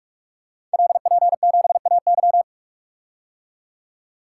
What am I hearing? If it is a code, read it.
Morse code, heard as a keyed tone, about 44 wpm, 700 Hz. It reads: LP7AY
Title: Code Groups - 5 Letter:Numbers - 45wpm